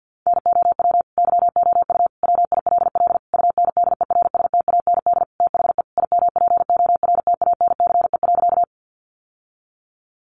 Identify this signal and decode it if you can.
Morse code, heard as a keyed tone, about 50 wpm, 700 Hz. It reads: NOW YOU WILL UNDERSTAND THE IMPORTANCE.
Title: Sentences from Top 1200 Words 50wpm